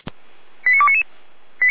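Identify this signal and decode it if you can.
Five-tone selective call (ZVEI), heard as quick runs of five tones, tones of about 70 ms. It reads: 89299
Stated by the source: ZVEI five-tone selective call received on an SDR.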